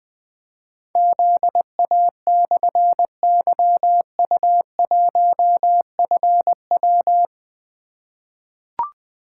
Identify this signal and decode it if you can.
Morse code, heard as a keyed tone, about 20 wpm, 700 Hz. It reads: ZA/YU1FW
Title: Difficult Call Signs 20wpm